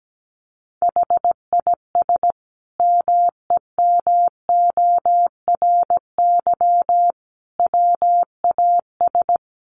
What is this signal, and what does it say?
Morse code, heard as a keyed tone, about 17 wpm, 700 Hz. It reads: HIS MEMORY WAS
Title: Sentences from Top 1300 Words 17wpm